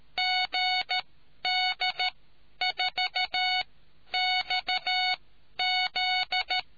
Morse code, heard as a keyed tone, about 13 wpm, spaced as for 10 wpm, 740 Hz. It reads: GD4XZ